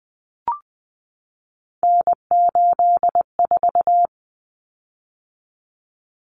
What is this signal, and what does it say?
Morse code, heard as a keyed tone, about 20 wpm, 700 Hz. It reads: N84
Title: Common Cut Numbers - 3-Digit 20wpm